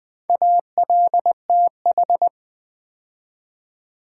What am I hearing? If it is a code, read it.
Morse code, heard as a keyed tone, about 20 wpm, 700 Hz. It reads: ALTH
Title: Build A Word - 4 Letters At A Time - 8-Letter Words - 20wpm